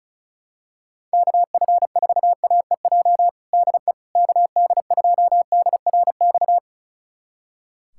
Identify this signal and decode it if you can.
Morse code, heard as a keyed tone, about 36 wpm, 700 Hz. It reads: KF4AEJ DE KD2DRX
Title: USA Callsign DE Callsign - 6 Characters - 35wpm